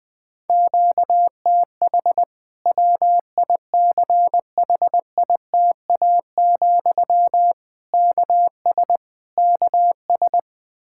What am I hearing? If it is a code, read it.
Morse code, heard as a keyed tone, about 20 wpm, 700 Hz. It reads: QTH WICHITA, KS KS
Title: Practice QSO - Exchange 1 - Beginning - 20wpm